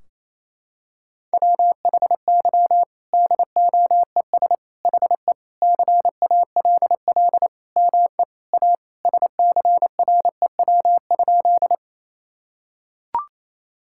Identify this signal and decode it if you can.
Morse code, heard as a keyed tone, about 28 wpm, 700 Hz. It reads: WHY DOES HE CALL ME A SCREW?
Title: Sentences from Top 1600 Words 28wpm